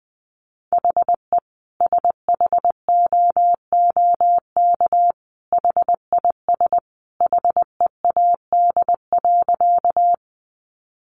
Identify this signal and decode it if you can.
Morse code, heard as a keyed tone, about 19 wpm, 700 Hz. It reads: HE SHOOK HIS HEAD.
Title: A Sound Of Thunder 20wpm